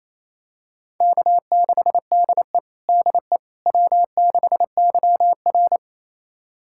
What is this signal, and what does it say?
Morse code, heard as a keyed tone, about 28 wpm, 700 Hz. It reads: K6DE DE W6YR
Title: USA Callsign DE Callsign - 4 Characters - 28wpm